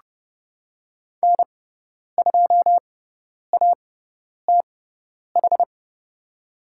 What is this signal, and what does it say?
Morse code, heard as a keyed tone, about 30 wpm, spaced as for 10 wpm, 700 Hz. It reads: N2ATH
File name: Lesson 18g - Letter m - Call Signs 30wpm